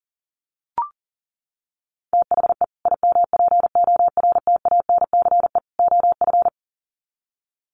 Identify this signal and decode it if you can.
Morse code, heard as a keyed tone, about 37 wpm, 700 Hz. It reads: THE IMPORTANCE OF
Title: Sets of 3 Words 40wpm